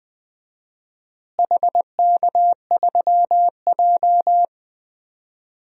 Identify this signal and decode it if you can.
Morse code, heard as a keyed tone, about 21 wpm, 700 Hz. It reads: HK3J
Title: DXCC-100 Call Signs - 4 Characters - Repeated 3x - 20wpm